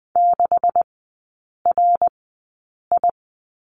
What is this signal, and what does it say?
Morse code, heard as a keyed tone, about 20 wpm, spaced as for 8 wpm, 700 Hz. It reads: 6RI
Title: 127 Lesson 21e - Number 6 - Three Characters 20wpm